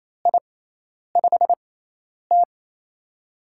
Morse code, heard as a keyed tone, about 27 wpm, spaced as for 10 wpm, 700 Hz. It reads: I5T
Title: USA Call Signs - 4 Characters - 6x Character Spacing - 28wpm